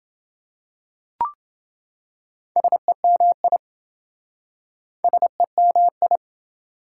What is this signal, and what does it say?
Morse code, heard as a keyed tone, about 27 wpm, 700 Hz. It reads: SEMI SEMI